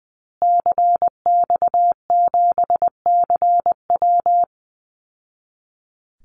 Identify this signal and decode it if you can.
Morse code, heard as a keyed tone, about 20 wpm, 700 Hz. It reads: CX7CW
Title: Call Signs Repeated 3x - 20wpm